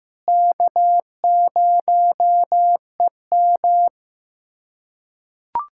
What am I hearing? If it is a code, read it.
Morse code, heard as a keyed tone, about 15 wpm, 700 Hz. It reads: K0EM
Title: USA Callsign DE Callsign - 4 Characters - 15wpm